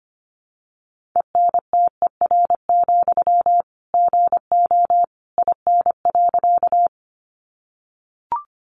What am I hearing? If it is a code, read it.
Morse code, heard as a keyed tone, about 25 wpm, 700 Hz. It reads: ENTER, GO IN.
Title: Sentences from Top 1300 Words 25wpm